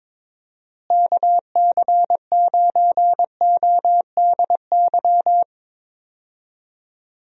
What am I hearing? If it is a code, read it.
Morse code, heard as a keyed tone, about 21 wpm, 700 Hz. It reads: KC9ODY
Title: USA Call Signs - 6 Characters - Repeated 3x - 22wpm